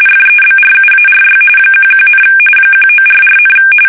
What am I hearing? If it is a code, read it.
RTTY, about 45.45 baud. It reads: FEBRUARY 11, 1944,
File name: fsk850 / battleshipnj